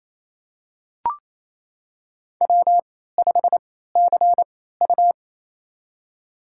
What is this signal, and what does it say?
Morse code, heard as a keyed tone, about 28 wpm, spaced as for 16 wpm, 700 Hz. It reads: W5CU
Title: USA Call Signs - 4 Characters - 3x Character Spacing - 28wpm